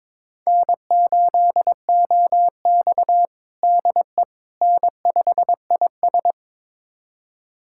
Morse code, heard as a keyed tone, about 22 wpm, 700 Hz. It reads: N8OX DE N5IS
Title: USA Callsign DE Callsign - 4 Characters - 22wpm